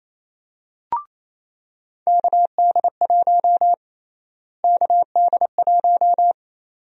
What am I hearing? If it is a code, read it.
Morse code, heard as a keyed tone, about 28 wpm, 700 Hz. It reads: KD1 KD1
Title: USA Call Sign Prefixes - Repeated 2x - 28wpm